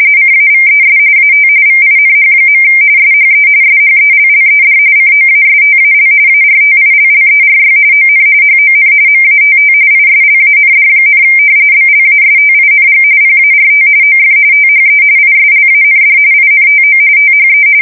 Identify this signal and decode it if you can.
RTTY, about 45.45 baud. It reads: CAVES AT VINH. SHE SHELLED COASTAL ARTILLERY ON HON MATT ISLAND, BLASTING PART OF THE ISLAND INTO THE
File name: fsk170 / battleshipnj